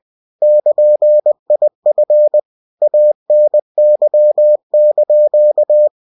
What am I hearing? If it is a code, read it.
Morse code, heard as a keyed tone, about 20 wpm, 600 Hz. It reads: <KN>IF ANY)
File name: cw-books / winnie-the-pooh / winnie-20wpm-page-0119